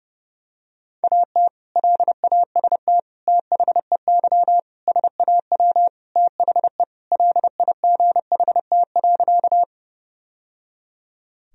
Morse code, heard as a keyed tone, about 30 wpm, 700 Hz. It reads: AT LAST THEY SAW THE LIGHT.
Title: Sentences from Top 1000 Words - Review Part 1 - 30wpm